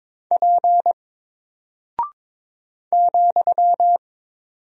Morse code, heard as a keyed tone, about 22 wpm, 700 Hz. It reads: P ,
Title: Single Character (Letter:Number:Punctuation:Prosign) 22wpm